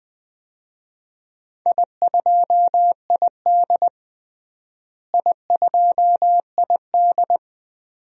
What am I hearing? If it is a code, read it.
Morse code, heard as a keyed tone, about 20 wpm, 700 Hz. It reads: I2ID I2ID
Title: DXCC-100 Call Signs - 4 Characters - Repeated 2x - 20wpm